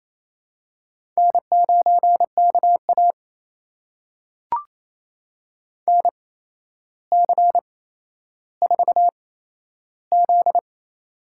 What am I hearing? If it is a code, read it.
Morse code, heard as a keyed tone, about 28 wpm, spaced as for 12 wpm, 700 Hz. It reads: N9KA NC4Z
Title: USA Call Signs - 4 Characters - 8x Character Spacing - 28wpm